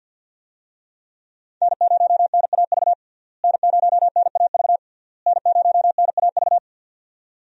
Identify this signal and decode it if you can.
Morse code, heard as a keyed tone, about 53 wpm, 690 Hz. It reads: N0NAV N0NAV N0NAV
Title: USA Call Signs - 5 Characters - Repeated 3x - 50wpm